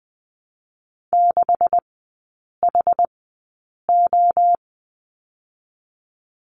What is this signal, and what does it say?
Morse code, heard as a keyed tone, about 20 wpm, spaced as for 8 wpm, 700 Hz. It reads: 6HO
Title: 152 Lesson 25e - Letter p - Three Characters 20wpm